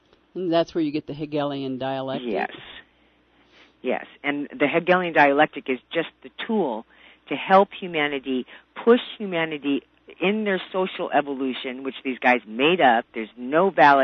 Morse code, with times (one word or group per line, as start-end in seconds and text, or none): none